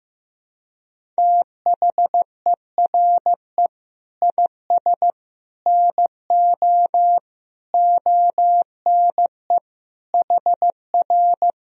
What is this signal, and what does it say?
Morse code, heard as a keyed tone, about 15 wpm, 700 Hz. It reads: THERE IS NO ONE HR
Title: Sentences from Top 200 Words and Abbreviations 15wpm